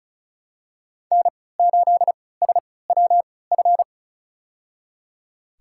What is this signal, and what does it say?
Morse code, heard as a keyed tone, about 34 wpm, spaced as for 20 wpm, 700 Hz. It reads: N8SWF
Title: USA Call Signs - 5 Characters - 3x Character Spacing - 35wpm